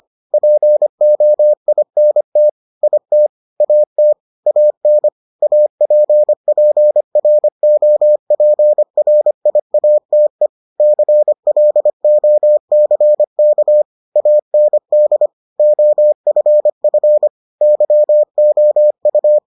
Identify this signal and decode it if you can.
Morse code, heard as a keyed tone, about 25 wpm, 600 Hz. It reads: POINT IT AT AN APPROPRIATE CLOCK AND OFF YOU